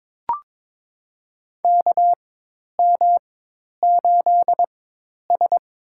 Morse code, heard as a keyed tone, about 22 wpm, spaced as for 10 wpm, 700 Hz. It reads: KM8S
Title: USA Call Signs - 4 Characters - 4x Character Spacing - 22wpm